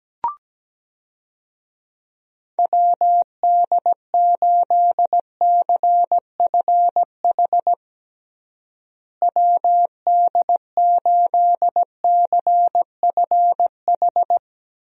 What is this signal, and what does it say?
Morse code, heard as a keyed tone, about 17 wpm, 700 Hz. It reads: WD8CFH WD8CFH
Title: USA Call Signs - 6 Characters - Repeated 2x - 17wpm